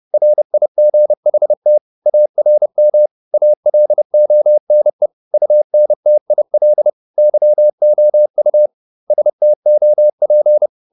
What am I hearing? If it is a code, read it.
Morse code, heard as a keyed tone, about 30 wpm, 600 Hz. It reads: RIGHT ARM ALONE UNTIL YOU STOP
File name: the old man and the sea / the old man and the sea 1 30wpm